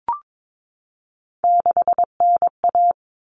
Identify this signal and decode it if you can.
Morse code, heard as a keyed tone, about 22 wpm, 700 Hz. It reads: 6NA
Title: Common Cut Numbers - 3-Digit 22wpm